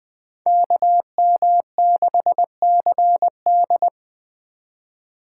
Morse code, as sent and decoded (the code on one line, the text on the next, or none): -.- -- -.... -.-. -..
KM6CD